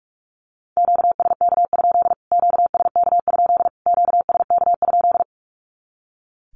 Morse code, heard as a keyed tone, about 44 wpm, 700 Hz. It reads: QSK? QSK? QSK?
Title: Q-Codes Repeated 3x 45wpm